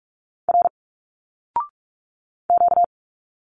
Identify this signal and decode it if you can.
Morse code, heard as a keyed tone, about 46 wpm, 700 Hz. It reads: R Q
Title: Single Letters 45wpm